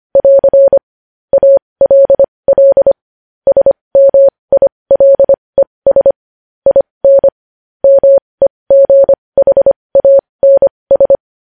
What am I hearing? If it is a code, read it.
Morse code, heard as a keyed tone, about 25 wpm, spaced as for 20 wpm, 550 Hz. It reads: <AR> ALL SMILES IN MEGHANS
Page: news-headlines-in-morse-code-at-20-wpm